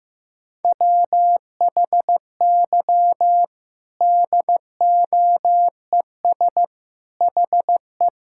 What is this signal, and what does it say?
Morse code, heard as a keyed tone, about 15 wpm, 700 Hz. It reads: WHY DOES HE
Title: Sentences from Top 1600 Words 15wpm